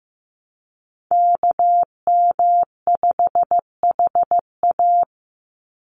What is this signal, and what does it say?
Morse code, heard as a keyed tone, about 15 wpm, 700 Hz. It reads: KM5HA
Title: USA Call Signs 15wpm